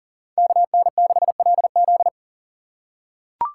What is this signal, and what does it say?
Morse code, heard as a keyed tone, about 40 wpm, 700 Hz. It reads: KN6LZ
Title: USA Call Signs - 5 Characters - 4x Character Spacing - 40wpm